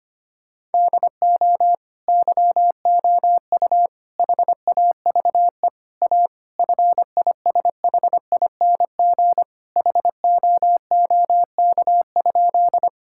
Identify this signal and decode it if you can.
Morse code, heard as a keyed tone, about 25 wpm, 700 Hz. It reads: DO YOU HAVE A FISHING HOOK?
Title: Sentences from Top 1600 Words 25wpm